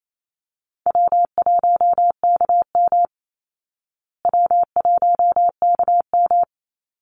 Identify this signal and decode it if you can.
Morse code, heard as a keyed tone, about 28 wpm, 700 Hz. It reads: W1KM W1KM
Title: Call Signs Repeated 2x - 28wpm